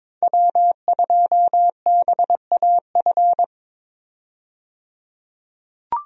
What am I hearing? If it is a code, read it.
Morse code, heard as a keyed tone, about 22 wpm, 700 Hz. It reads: W2BAF
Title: USA Call Signs - 5 Characters - 22wpm